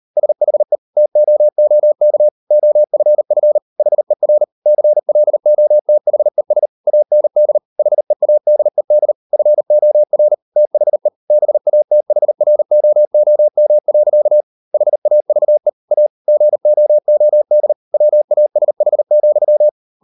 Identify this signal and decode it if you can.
Morse code, heard as a keyed tone, about 38 wpm, 600 Hz. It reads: SHE TOOK OFF HER CLOTHES AND HEADED FOR THE BATHROOM. HAVE A GOOD WASH,